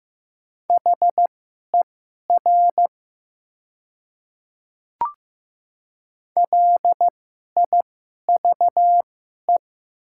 Morse code, heard as a keyed tone, about 15 wpm, spaced as for 11 wpm, 700 Hz. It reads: HER LIVE
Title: 2x Character Spacing - Top 200 Words - 15wpm